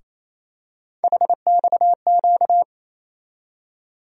Morse code, heard as a keyed tone, about 28 wpm, 700 Hz. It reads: HXQ
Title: USA Call Sign Postfixes - 3 Letters 28wpm